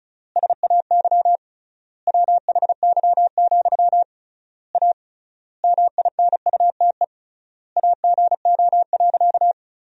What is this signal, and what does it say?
Morse code, heard as a keyed tone, about 34 wpm, 700 Hz. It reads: SAY WHY, A MINUTE AGO.
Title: 3x Word-Spacing - Sentences from Top 500 Words - 35wpm